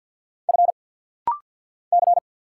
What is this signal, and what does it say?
Morse code, heard as a keyed tone, about 50 wpm, 700 Hz. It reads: F C
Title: Single Letter-Number - Rapid-Fire 50wpm